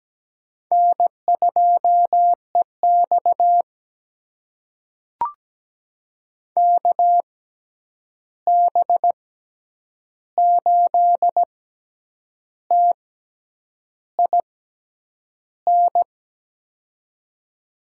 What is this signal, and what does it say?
Morse code, heard as a keyed tone, about 17 wpm, spaced as for 8 wpm, 700 Hz. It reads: N2EX KB8TIN
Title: USA Call Signs - 6x Character Spacing - 17wpm